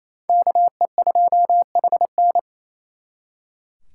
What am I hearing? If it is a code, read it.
Morse code, heard as a keyed tone, about 35 wpm, 700 Hz. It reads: KE2HN